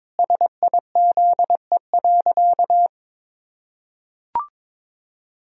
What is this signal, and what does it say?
Morse code, heard as a keyed tone, about 22 wpm, 700 Hz. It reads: SIZE.
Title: Sentences from Top 1000 Words - Review Part 2 - 22wpm